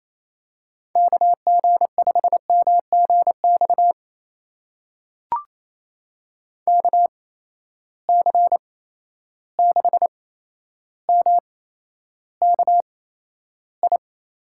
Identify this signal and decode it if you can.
Morse code, heard as a keyed tone, about 29 wpm, spaced as for 12 wpm, 700 Hz. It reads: KG5MGX KC6MKI
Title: USA Call Signs - 6 Characters - 8x Character Spacing - 28wpm